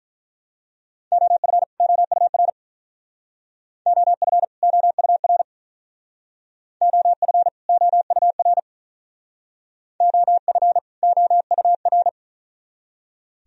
Morse code, heard as a keyed tone, about 42 wpm, 700 Hz. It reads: OF OUR OF OUR OF OUR OF OUR
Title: Speed-Racing - Sets of 2 Words 35wpm